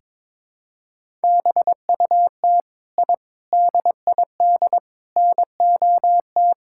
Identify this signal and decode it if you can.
Morse code, heard as a keyed tone, about 22 wpm, 700 Hz. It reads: BUT I DID NOT
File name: practice / 22wpm / Sentences with 4-Word Phrases - 22wpm